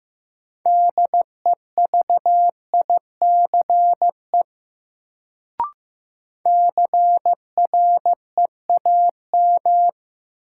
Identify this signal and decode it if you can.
Morse code, heard as a keyed tone, about 16 wpm, 700 Hz. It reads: DEVICE CREAM
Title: Sentences from Top 1500 Words - New Word Drill - 15wpm